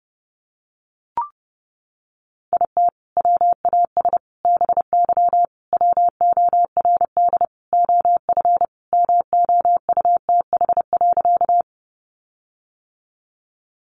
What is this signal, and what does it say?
Morse code, heard as a keyed tone, about 30 wpm, 700 Hz. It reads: IT WAS BY WORD OF MOUTH.
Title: Sentences from Top 1500 Words 30wpm